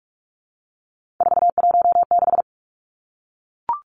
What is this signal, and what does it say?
Morse code, heard as a keyed tone, about 44 wpm, 700 Hz. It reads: 416
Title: Numbers - 3-Digit 45wpm